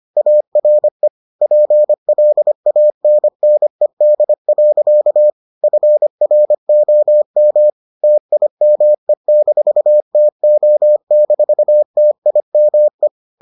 Morse code, heard as a keyed tone, about 25 wpm, 600 Hz. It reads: ARE PLANNED. FROM TIME-TO-TIME